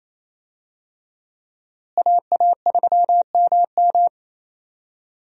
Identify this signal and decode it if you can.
Morse code, heard as a keyed tone, about 28 wpm, 700 Hz. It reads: AA3MM
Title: USA Call Signs - 5 Characters - Repeated 3x - 28wpm